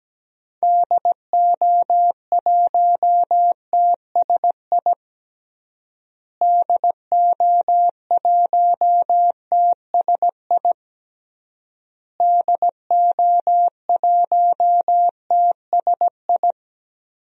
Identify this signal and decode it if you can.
Morse code, heard as a keyed tone, about 17 wpm, 700 Hz. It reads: DO1TSI DO1TSI DO1TSI
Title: DXCC-100 Call Signs - 6 Characters - Repeated 3x - 17wpm